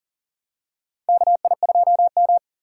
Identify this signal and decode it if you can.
Morse code, heard as a keyed tone, about 37 wpm, 700 Hz. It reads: KI2M
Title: USA Call Signs - 2x Character Spacing - 40wpm